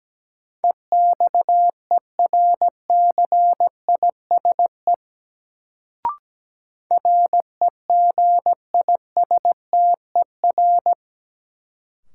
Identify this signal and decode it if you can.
Morse code, heard as a keyed tone, about 17 wpm, 700 Hz. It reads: EXERCISE REGISTER
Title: Sentences from Top 1200 Words - New Word Drill Alt - 17wpm